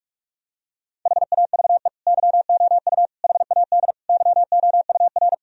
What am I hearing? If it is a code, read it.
Morse code, heard as a keyed tone, about 41 wpm, 700 Hz. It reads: HAVE YOU HAD YOUR
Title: Sentences from Top 1000 Words - Review Part 3 - 45wpm